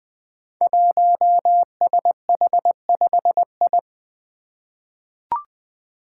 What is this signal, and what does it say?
Morse code, heard as a keyed tone, about 20 wpm, 700 Hz. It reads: 1SH5I
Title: Five Challenging Letter-Number Combinations 20wpm